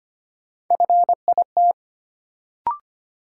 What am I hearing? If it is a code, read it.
Morse code, heard as a keyed tone, about 25 wpm, 700 Hz. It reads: FIT
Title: Sentences from Top 800 Words - New Word Drill - 25wpm